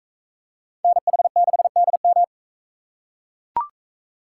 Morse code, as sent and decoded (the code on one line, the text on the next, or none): -. .... -.... -.. --
NH6DM